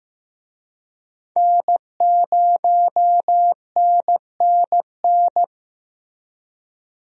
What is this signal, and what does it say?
Morse code, heard as a keyed tone, about 15 wpm, 700 Hz. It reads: N0NNN
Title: USA Call Signs - 5 Characters - Repeated 3x - 15wpm